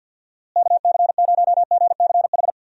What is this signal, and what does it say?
Morse code, heard as a keyed tone, about 50 wpm, 700 Hz. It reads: KC0GCH
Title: USA Call Signs - 6 Characters - Repeated 3x - 50wpm